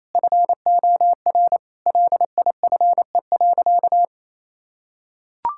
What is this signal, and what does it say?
Morse code, heard as a keyed tone, about 28 wpm, 700 Hz. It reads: FOR LIFE.
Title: Sentences from Top 1000 Words - Review Part 6 - 28wpm